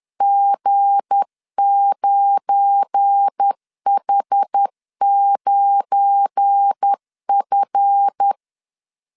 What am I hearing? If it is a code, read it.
Morse code, heard as a keyed tone, about 10 wpm, 800 Hz. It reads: G9H9F
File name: LEZIONI / 7 GF9Z / 7 GF9 10wpm 2747